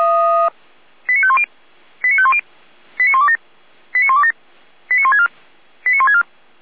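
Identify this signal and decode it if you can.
Five-tone selective call (ZVEI), heard as quick runs of five tones, tones of about 70 ms. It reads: 89510 89510 89127 89127 89164 89164
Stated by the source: ZVEI five-tone selective call received by SDR, from the Feuerwehr Orlamünde fire brigade audio server.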